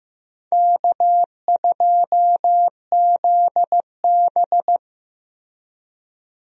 Morse code, heard as a keyed tone, about 15 wpm, 700 Hz. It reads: K2ZB
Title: USA Call Signs - 4 Characters - Repeated 3x - 15wpm